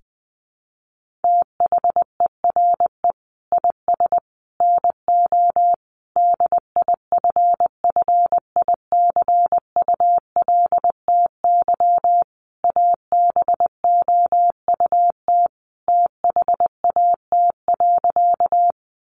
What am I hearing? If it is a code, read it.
Morse code, heard as a keyed tone, about 19 wpm, 700 Hz. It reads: THERE IS NO DIFFICULTY ABOUT THAT.
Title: Sentences from Top 1500 Words 20wpm